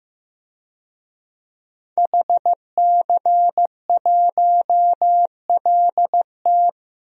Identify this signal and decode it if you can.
Morse code, heard as a keyed tone, about 15 wpm, 700 Hz. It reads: HC1LT
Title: DXCC-100 Call Signs - 5 Characters - 15wpm